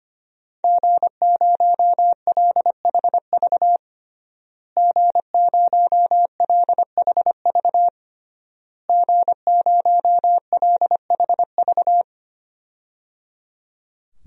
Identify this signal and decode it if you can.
Morse code, heard as a keyed tone, about 25 wpm, 700 Hz. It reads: G0LHV G0LHV G0LHV
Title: DXCC-100 Call Signs - 5 Characters - Repeated 3x - 25wpm